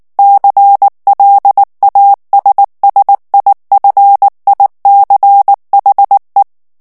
Morse code, heard as a keyed tone, about 19 wpm, 800 Hz. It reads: CLASSIFICHE
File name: morse mp3 / ITALIEN texte clair 2 vitesse 18 mots minute